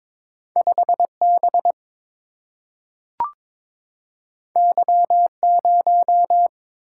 Morse code, heard as a keyed tone, about 22 wpm, 700 Hz. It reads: 5 B Y 0
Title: Single-Letter Single-Number 22wpm